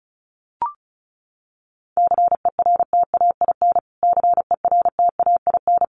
Morse code, heard as a keyed tone, about 34 wpm, 700 Hz. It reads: CERTAIN CERTAIN
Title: Top 500 Words Repeated 35wpm